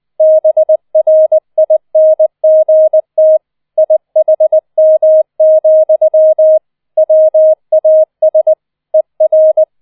Morse code, heard as a keyed tone, about 20 wpm, 630 Hz. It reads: BRINGT IHM, WAS ER